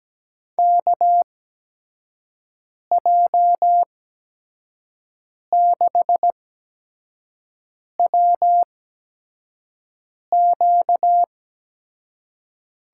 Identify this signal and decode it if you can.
Morse code, heard as a keyed tone, about 17 wpm, spaced as for 5 wpm, 700 Hz. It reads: KJ6WQ
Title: USA Call Signs - 8x Character Spacing - 17wpm